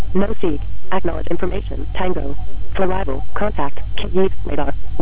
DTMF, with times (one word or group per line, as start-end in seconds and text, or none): none